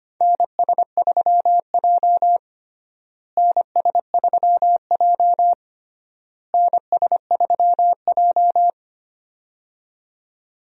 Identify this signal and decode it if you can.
Morse code, heard as a keyed tone, about 25 wpm, 700 Hz. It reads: NS3J NS3J NS3J
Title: USA Call Signs - 4 Characters - Repeated 3x - 25wpm